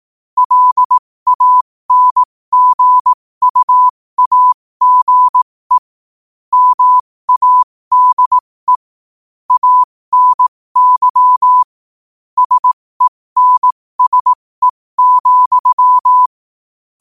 Morse code, heard as a keyed tone, about 18 wpm, spaced as for 15 wpm, 1000 Hz. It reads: LANGUAGE MADE ANY SENSE,
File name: qotdfiles / 15wpm